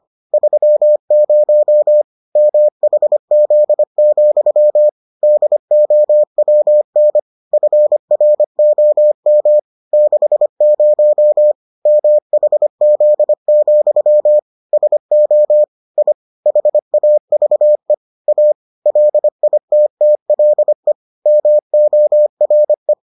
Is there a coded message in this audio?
Yes: Morse code from 0.3 to 22.9 s, heard as a keyed tone, about 25 wpm, 600 Hz. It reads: 30 MHZ, DOWN FROM 60 MHZ, SO I HAVE A LITTLE MORE